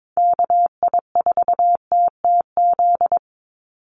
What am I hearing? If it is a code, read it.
Morse code, heard as a keyed tone, about 22 wpm, 700 Hz. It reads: KI4TTZ
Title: USA Call Signs 22wpm